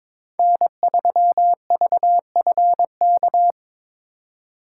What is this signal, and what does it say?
Morse code, heard as a keyed tone, about 26 wpm, 700 Hz. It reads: N3VFK
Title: Speed Racing - USA 1x3 Call Signs 22wpm